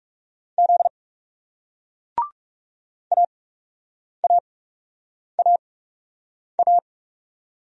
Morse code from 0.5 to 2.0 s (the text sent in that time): Z